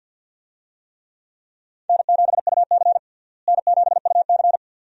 Morse code, heard as a keyed tone, about 52 wpm, 700 Hz. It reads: N7UC N7UC
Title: USA Call Signs - 4 Characters - Repeated 2x - 50wpm